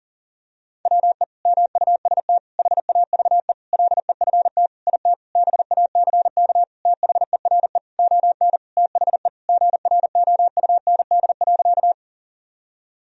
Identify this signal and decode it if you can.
Morse code, heard as a keyed tone, about 37 wpm, 700 Hz. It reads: WE MUST HAVE LEFT IT BACK THERE ON THE GROUND.
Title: Sentences from Top 1000 Words - Review Part 2 - 40wpm